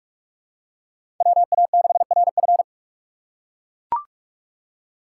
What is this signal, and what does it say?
Morse code, heard as a keyed tone, about 42 wpm, 700 Hz. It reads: WA6RF
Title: USA Call Signs - 5 Characters - 2x Character Spacing - 45wpm